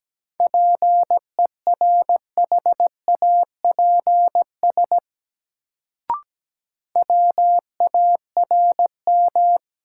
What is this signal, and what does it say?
Morse code, heard as a keyed tone, about 17 wpm, 700 Hz. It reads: PERHAPS WARM
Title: Sentences from Top 500 Words - New Word Drill - 17wpm